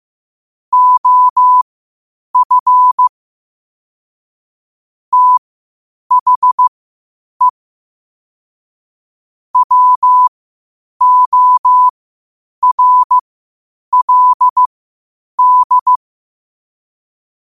Morse code, heard as a keyed tone, about 15 wpm, spaced as for 8 wpm, 1000 Hz. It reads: OF THE WORLD
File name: qotdfiles / 8wpm